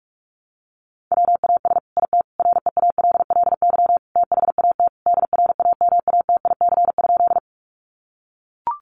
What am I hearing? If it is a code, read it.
Morse code, heard as a keyed tone, about 44 wpm, 700 Hz. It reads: WAS IT REALLY THAT DRAMATIC?
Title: Sentences from Top 1300 Words 45wpm